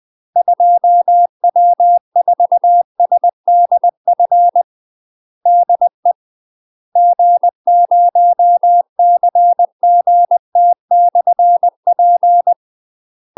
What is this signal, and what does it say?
Morse code, heard as a keyed tone, about 20 wpm, 700 Hz. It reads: 2W4SDF DE G0CGT/P
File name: UK MASTER QSOs TRAINING / 20WPM / UK MASTER QSOs 0069